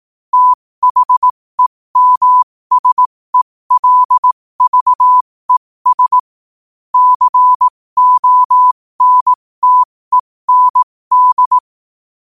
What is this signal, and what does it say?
Morse code, heard as a keyed tone, about 18 wpm, spaced as for 15 wpm, 1000 Hz. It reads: THEMSELVES CONTEND
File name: qotdfiles / 15wpm